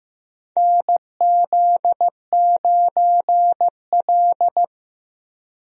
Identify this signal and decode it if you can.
Morse code, heard as a keyed tone, about 15 wpm, 700 Hz. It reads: NZ9L
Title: USA Call Signs - 4 Characters - Repeated 3x - 15wpm